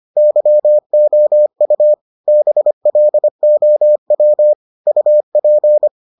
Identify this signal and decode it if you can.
Morse code, heard as a keyed tone, about 25 wpm, 600 Hz. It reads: YOU BLOW UP